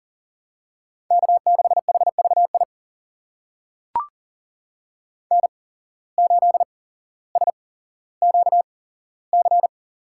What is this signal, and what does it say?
Morse code, heard as a keyed tone, about 40 wpm, spaced as for 17 wpm, 700 Hz. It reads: K6HVI N8SQC
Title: USA Call Signs - 5 Characters - 8x Character Spacing - 40wpm